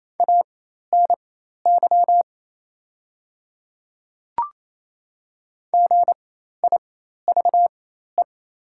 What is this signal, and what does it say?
Morse code, heard as a keyed tone, about 28 wpm, spaced as for 13 wpm, 700 Hz. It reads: ANY GIVE
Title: 4x Character Spacing - Top 500 Words - 28wpm